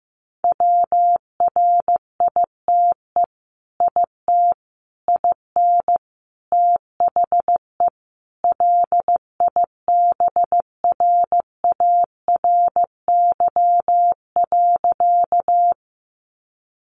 Morse code, heard as a keyed tone, about 15 wpm, 700 Hz. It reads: WRITE IT IN THE LIBRARY.
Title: Sentences from Top 600 Words 15wpm